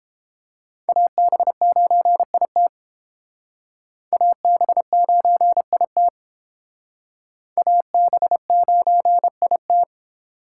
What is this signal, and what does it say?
Morse code, heard as a keyed tone, about 29 wpm, 700 Hz. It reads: AB9IT AB9IT AB9IT